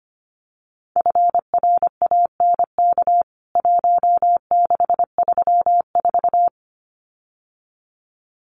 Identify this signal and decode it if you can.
Morse code, heard as a keyed tone, about 25 wpm, 700 Hz. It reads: FRANK 1634
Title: CWT Exchange 25wpm